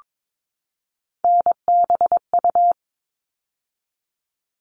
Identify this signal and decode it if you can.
Morse code, heard as a keyed tone, about 22 wpm, 700 Hz. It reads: NBU
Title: All Cut Numbers - 3-Digit 22wpm